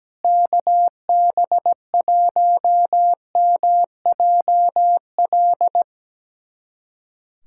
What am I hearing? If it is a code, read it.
Morse code, heard as a keyed tone, about 17 wpm, 700 Hz. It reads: KB1MJL
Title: USA Callsign DE Callsign - 6 Characters - 17wpm